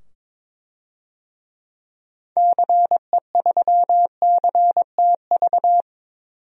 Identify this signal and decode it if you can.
Morse code, heard as a keyed tone, about 22 wpm, 700 Hz. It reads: CE3CTV
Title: DXCC-100 Call Signs - 6 Characters - 22wpm